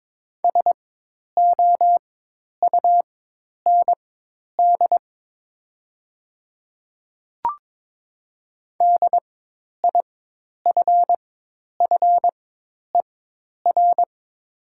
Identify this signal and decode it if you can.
Morse code, heard as a keyed tone, about 22 wpm, spaced as for 10 wpm, 700 Hz. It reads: SOUND DIFFER
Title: 4x Character Spacing - Top 200 Words - 22wpm